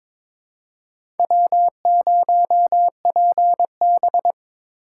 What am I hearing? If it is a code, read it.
Morse code, heard as a keyed tone, about 22 wpm, 700 Hz. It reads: W0PB
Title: USA Call Signs - 2x Character Spacing - 22wpm